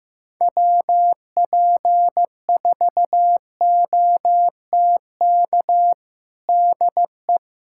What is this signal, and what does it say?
Morse code, heard as a keyed tone, about 15 wpm, 700 Hz. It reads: WP4OTK DE
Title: USA Callsign DE Callsign - 6 Characters - 15wpm